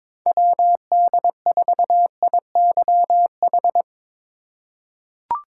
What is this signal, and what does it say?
Morse code, heard as a keyed tone, about 22 wpm, 700 Hz. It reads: WD4IYH
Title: USA Call Signs - 6 Characters - 2x Character Spacing - 22wpm